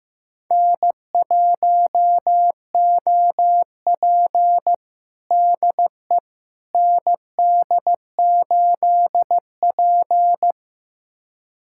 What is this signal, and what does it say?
Morse code, heard as a keyed tone, about 15 wpm, 700 Hz. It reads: N1OP DE ND8P
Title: USA Callsign DE Callsign - 4 Characters - 15wpm